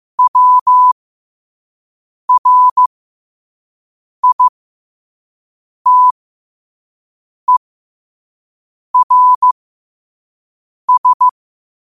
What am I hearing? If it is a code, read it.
Morse code, heard as a keyed tone, about 15 wpm, spaced as for 5 wpm, 1000 Hz. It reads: WRITERS
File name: qotdfiles / 5wpm